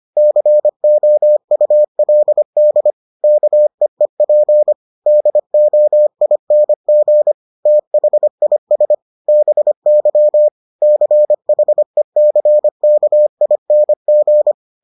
Morse code, heard as a keyed tone, about 25 wpm, 600 Hz. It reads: COULD KEEP DOING THIS BY CHECKING